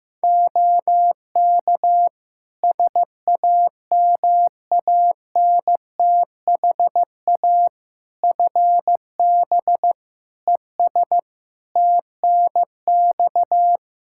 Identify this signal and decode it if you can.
Morse code, heard as a keyed tone, about 15 wpm, 700 Hz. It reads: OK SAMANTHA FB ES TNX
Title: Practice QSO - Exchange 2 - Info - 15wpm